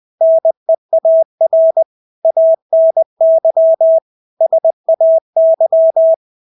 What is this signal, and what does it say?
Morse code, heard as a keyed tone, about 20 wpm, 650 Hz. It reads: NEAR ANY SAY
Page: Engelske ord 20 wpm